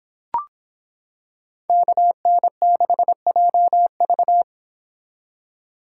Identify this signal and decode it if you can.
Morse code, heard as a keyed tone, about 26 wpm, 700 Hz. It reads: KN6JV